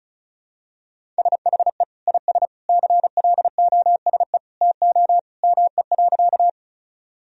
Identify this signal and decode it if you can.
Morse code, heard as a keyed tone, about 34 wpm, 700 Hz. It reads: SHE IS CLOSE TO ME.
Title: Sentences from Top 1000 Words - Review Part 1 - 35wpm